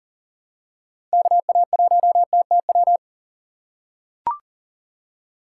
Morse code, heard as a keyed tone, about 38 wpm, 700 Hz. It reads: KA1TTW
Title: USA Call Signs - 6 Characters - 2x Character Spacing - 40wpm